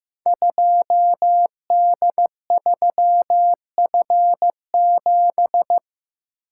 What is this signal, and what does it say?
Morse code, heard as a keyed tone, about 15 wpm, 700 Hz. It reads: 2D3F7
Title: Code Groups - 5 Letter:Numbers - 15wpm